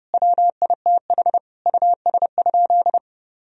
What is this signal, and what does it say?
Morse code, heard as a keyed tone, about 30 wpm, 700 Hz. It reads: WITH US?
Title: Sentences from Top 1000 Words - Review Part 3 - 30wpm